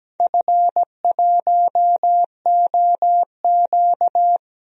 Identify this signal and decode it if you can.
Morse code, heard as a keyed tone, about 17 wpm, 700 Hz. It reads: F1OQ
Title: DXCC-100 Call Signs - 4 Characters - 17wpm